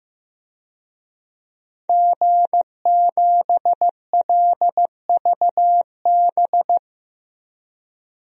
Morse code, heard as a keyed tone, about 16 wpm, 700 Hz. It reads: G7LVB
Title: DXCC-100 Call Signs - 5 Characters - 15wpm